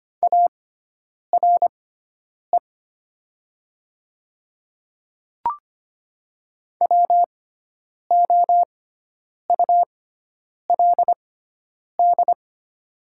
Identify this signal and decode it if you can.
Morse code, heard as a keyed tone, about 25 wpm, spaced as for 9 wpm, 700 Hz. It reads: ARE WOULD
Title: 6x Character Spacing - Top 100 Words - 25wpm